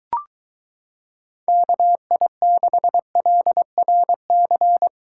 Morse code, heard as a keyed tone, about 23 wpm, 700 Hz. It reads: KI6LRC